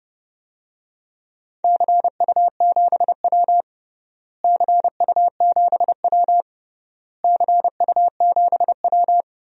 Morse code, heard as a keyed tone, about 30 wpm, 700 Hz. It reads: CU7W CU7W CU7W
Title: DXCC-100 Call Signs - 4 Characters - Repeated 3x - 30wpm